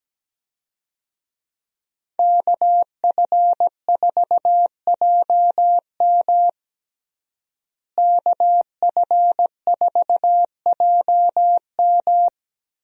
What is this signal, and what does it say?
Morse code, heard as a keyed tone, about 17 wpm, 700 Hz. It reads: KF4JM KF4JM
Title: USA Call Signs - 5 Characters - Repeated 3x - 17wpm